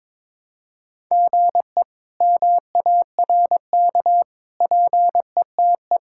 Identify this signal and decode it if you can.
Morse code, heard as a keyed tone, about 22 wpm, 700 Hz. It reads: GE MARK PETE
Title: K1USN SST QSO Practice - 22wpm